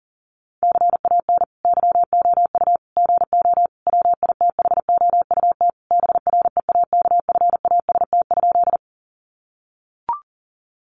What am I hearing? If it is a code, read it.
Morse code, heard as a keyed tone, about 40 wpm, 700 Hz. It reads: CAN YOU GO WITHOUT BREAKFAST?
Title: Sentences from Top 1500 Words 40wpm